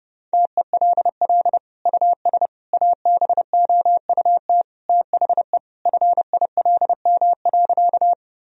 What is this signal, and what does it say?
Morse code, heard as a keyed tone, about 30 wpm, 700 Hz. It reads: TELL US ABOUT THE FILM.
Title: Sentences from Top 1000 Words - Review Part 3 - 30wpm